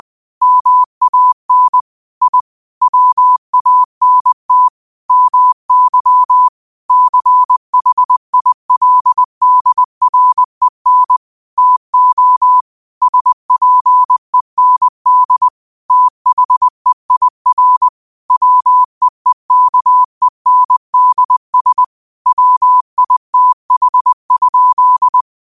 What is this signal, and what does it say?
Morse code, heard as a keyed tone, about 20 wpm, 1000 Hz. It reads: MAN I WANT MY CHILDREN TO SPEND THEIR WEEKENDS WITH?